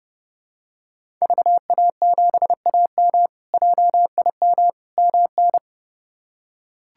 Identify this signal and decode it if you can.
Morse code, heard as a keyed tone, about 30 wpm, 700 Hz. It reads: VA7AM JIM MN
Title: K1USN SST QSO Practice - 30wpm